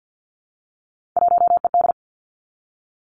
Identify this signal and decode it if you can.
Morse code, heard as a keyed tone, about 50 wpm, 700 Hz. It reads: 1ED
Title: Three Challenging Letter-Number Combinations 50wpm